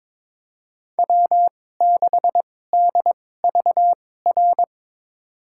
Morse code, heard as a keyed tone, about 22 wpm, spaced as for 16 wpm, 700 Hz. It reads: W6DVR
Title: USA Call Signs - 2x Character Spacing - 22wpm